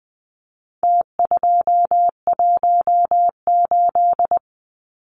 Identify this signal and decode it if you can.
Morse code, heard as a keyed tone, about 20 wpm, 700 Hz. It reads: T218
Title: Numbers - Contest Style - 4-Digit 20wpm